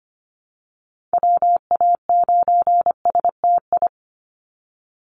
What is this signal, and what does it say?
Morse code, heard as a keyed tone, about 25 wpm, 690 Hz. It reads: WA9STI
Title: NA SOTA Activator Callsigns 25wpm